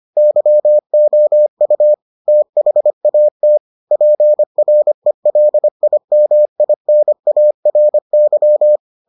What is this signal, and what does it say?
Morse code, heard as a keyed tone, about 25 wpm, 600 Hz. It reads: YOU THAT PRELIMINARY